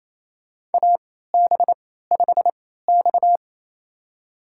Morse code, heard as a keyed tone, about 28 wpm, spaced as for 16 wpm, 700 Hz. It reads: AB5X
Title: USA Call Signs - 4 Characters - 3x Character Spacing - 28wpm